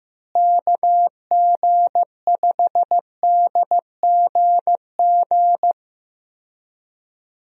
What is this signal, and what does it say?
Morse code, heard as a keyed tone, about 15 wpm, 700 Hz. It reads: KG5DGG
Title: USA Call Signs - 6 Characters - Repeated 3x - 15wpm